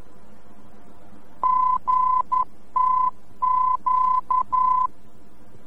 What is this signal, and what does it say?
Morse code, heard as a keyed tone, about 11 wpm, 1010 Hz. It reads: GTQ
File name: france / mx / GTQ-VOR-F